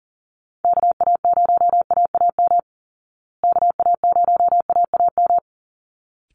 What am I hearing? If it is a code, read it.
Morse code, heard as a keyed tone, about 39 wpm, 700 Hz. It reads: KA0AAM KA0AAM
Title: Call Signs Repeated 3x - 40wpm